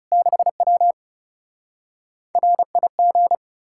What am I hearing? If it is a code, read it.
Morse code, heard as a keyed tone, about 37 wpm, 700 Hz. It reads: 6W RIG
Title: Speed-Racing Common QSO Phrases - 30wpm